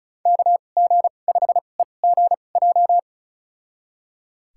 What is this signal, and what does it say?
Morse code, heard as a keyed tone, about 36 wpm, spaced as for 25 wpm, 700 Hz. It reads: KG5EGJ
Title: USA Call Signs - 6 Characters - 2x Character Spacing - 35wpm